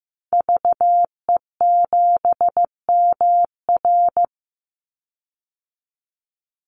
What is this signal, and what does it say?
Morse code, heard as a keyed tone, about 15 wpm, 700 Hz. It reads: VE7MR
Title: Call Signs Repeated 2x - 15wpm